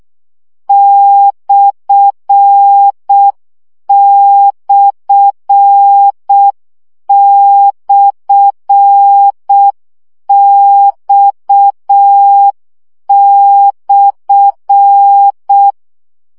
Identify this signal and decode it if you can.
Morse code, heard as a keyed tone, about 6 wpm, 800 Hz. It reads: ///X/
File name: CW gyakorlas x per 5 baud